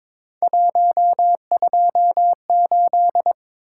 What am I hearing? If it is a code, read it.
Morse code, heard as a keyed tone, about 21 wpm, 700 Hz. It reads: 128
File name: practice / 22wpm / Three Letter-Number Combinations - 22wpm